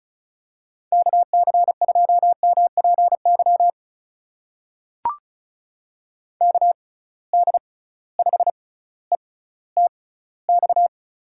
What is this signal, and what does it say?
Morse code, heard as a keyed tone, about 36 wpm, spaced as for 18 wpm, 700 Hz. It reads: KC2MPY KD5ETX
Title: USA Call Signs - 6 Characters - 6x Character Spacing - 35wpm